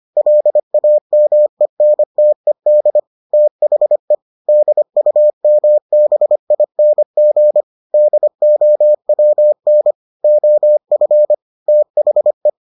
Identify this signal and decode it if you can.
Morse code, heard as a keyed tone, about 25 wpm, 600 Hz. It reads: LAMENTED THE DUMBING DOWN OF THE